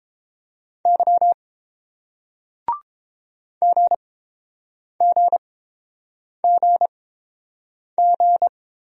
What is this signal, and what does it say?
Morse code, heard as a keyed tone, about 31 wpm, spaced as for 7 wpm, 700 Hz. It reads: Y GGGG